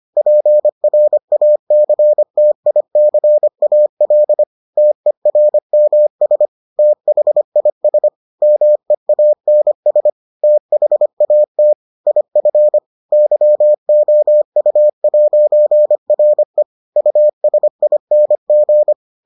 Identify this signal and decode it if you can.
Morse code, heard as a keyed tone, about 25 wpm, 600 Hz. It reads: PRACTICAL TERMS THIS MEANS THAT IF YOU'RE USING